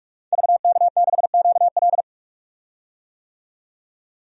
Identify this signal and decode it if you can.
Morse code, heard as a keyed tone, about 46 wpm, 700 Hz. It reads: VK6QL
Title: DXCC-100 Call Signs - 5 Characters - 45wpm